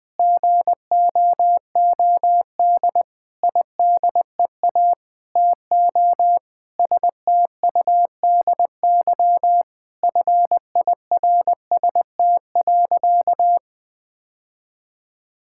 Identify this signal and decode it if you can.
Morse code, heard as a keyed tone, about 21 wpm, 700 Hz. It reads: GOOD IDEA TO STUDY FIRST.
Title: Sentences with 7-Word Phrases - 20wpm